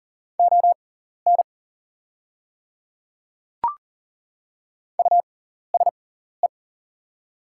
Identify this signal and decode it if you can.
Morse code, heard as a keyed tone, about 40 wpm, spaced as for 14 wpm, 690 Hz. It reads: ON USE
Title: 6x Character Spacing - Top 200 Words - 40wpm